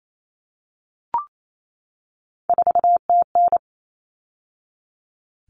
Morse code, heard as a keyed tone, about 28 wpm, 700 Hz. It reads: 4TN
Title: All Cut Numbers - 3-Digit 28wpm